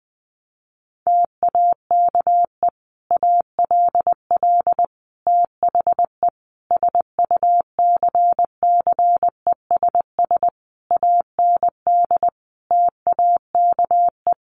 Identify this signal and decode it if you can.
Morse code, heard as a keyed tone, about 19 wpm, 700 Hz. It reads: TAKE ALL THE SUCCESS AND TAKE
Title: Sentences from Top 900 Words 20wpm